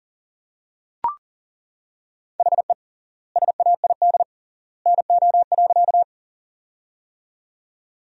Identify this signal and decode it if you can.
Morse code, heard as a keyed tone, about 39 wpm, 700 Hz. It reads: HE SAID NO.
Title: 3x Word-Spacing - Sentences from Top 100 Words - 40wpm